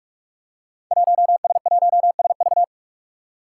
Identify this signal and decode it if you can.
Morse code, heard as a keyed tone, about 46 wpm, 700 Hz. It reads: 1S1SV
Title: Five Challenging Letter-Number Combinations 45wpm